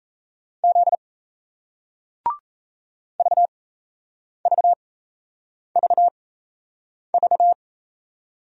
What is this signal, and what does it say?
Morse code, heard as a keyed tone, about 38 wpm, spaced as for 8 wpm, 700 Hz. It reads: Z VVVV